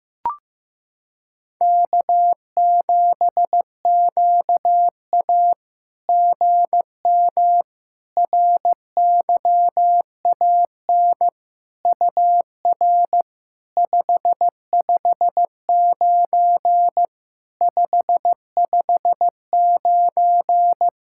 Morse code, heard as a keyed tone, about 15 wpm, 700 Hz. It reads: K7QA GM RYAN UR 559 559
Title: SOTA QSO Practice 15wpm